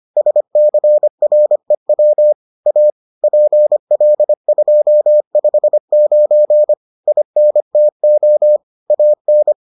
Morse code, heard as a keyed tone, about 25 wpm, 600 Hz. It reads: SCREW A PL259 INTO AN